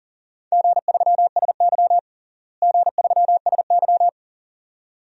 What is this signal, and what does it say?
Morse code, heard as a keyed tone, about 37 wpm, 700 Hz. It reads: G3SY G3SY
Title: DXCC-100 Call Signs - 4 Characters - Repeated 2x - 40wpm